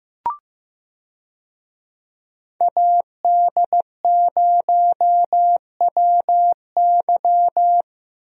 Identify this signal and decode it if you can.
Morse code, heard as a keyed tone, about 15 wpm, 700 Hz. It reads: AD0WY
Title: USA Call Signs - 5 Characters - Repeated 3x - 15wpm